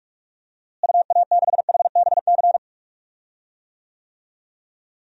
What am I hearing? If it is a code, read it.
Morse code, heard as a keyed tone, about 42 wpm, 700 Hz. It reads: UA6HBC
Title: DXCC-100 Call Signs - 6 Characters - Repeated 2x - 45wpm